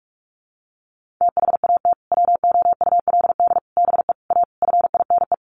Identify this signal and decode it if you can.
Morse code, heard as a keyed tone, about 44 wpm, 700 Hz. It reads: THAT WOULD BE A FINE
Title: Sentences from Top 500 Words 45wpm